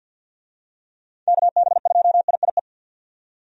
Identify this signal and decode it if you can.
Morse code, heard as a keyed tone, about 51 wpm, 700 Hz. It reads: KB2IIE
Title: USA Call Signs - 6 Characters - 50wpm